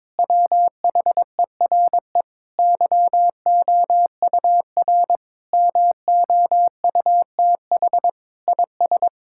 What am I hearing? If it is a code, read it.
Morse code, heard as a keyed tone, about 22 wpm, 700 Hz. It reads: WHERE YOUR MOUTH IS
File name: practice / 22wpm / Sentences with 8-Word Phrases - 22wpm